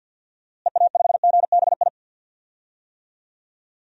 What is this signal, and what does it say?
Morse code, heard as a keyed tone, about 47 wpm, 700 Hz. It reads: EA5GBI
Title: DXCC-100 Call Signs - 6 Characters - Repeated 3x - 50wpm